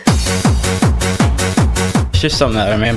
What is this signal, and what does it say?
There is no code.